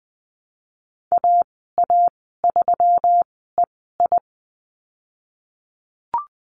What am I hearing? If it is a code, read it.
Morse code, heard as a keyed tone, about 20 wpm, spaced as for 15 wpm, 700 Hz. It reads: AA3EI
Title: 128 Lesson 21g - Number 6 - Call Signs 20wpm